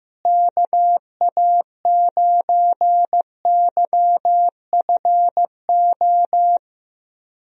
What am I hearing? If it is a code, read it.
Morse code, heard as a keyed tone, about 15 wpm, 700 Hz. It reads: KA9YFO
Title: USA Call Signs - 6 Characters - Repeated 2x - 15wpm